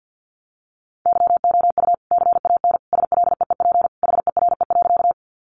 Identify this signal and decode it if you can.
Morse code, heard as a keyed tone, about 50 wpm, 700 Hz. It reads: YOU CAN SLEEP HERE.
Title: Sentences from Top 500 Words 50wpm